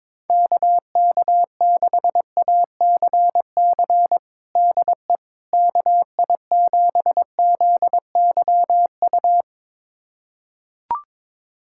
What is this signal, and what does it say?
Morse code, heard as a keyed tone, about 22 wpm, 700 Hz. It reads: KK6ACC DE KI7ZYU
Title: USA Callsign DE Callsign - 6 Characters - 22wpm